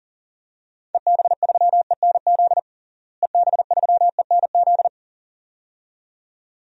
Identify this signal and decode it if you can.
Morse code, heard as a keyed tone, about 40 wpm, 700 Hz. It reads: EB3ENZ EB3ENZ
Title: DXCC-100 Call Signs - 6 Characters - Repeated 2x - 40wpm